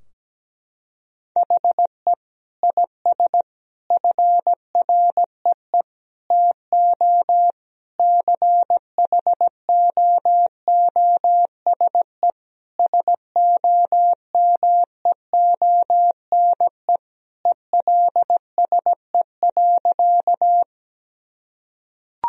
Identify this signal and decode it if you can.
Morse code, heard as a keyed tone, about 17 wpm, 700 Hz. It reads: HE IS FREE TO CHOOSE SOMEONE ELSE.
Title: Sentences from Top 1000 Words - Review Part 4 - 17wpm